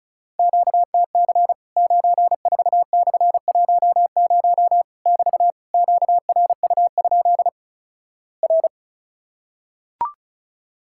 Morse code, heard as a keyed tone, about 36 wpm, 700 Hz. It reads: QTC 94/10 <BT> QRU? R
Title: Worked All Europe DX Contest - QTC Exchange - 35wpm